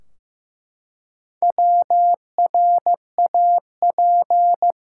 Morse code, heard as a keyed tone, about 15 wpm, 700 Hz. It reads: WRAP
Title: Four Letter Words - 15wpm